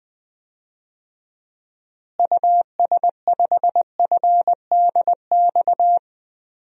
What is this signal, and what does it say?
Morse code, heard as a keyed tone, about 21 wpm, 700 Hz. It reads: US5FDX
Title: DXCC-100 Call Signs - 6 Characters - Repeated 2x - 20wpm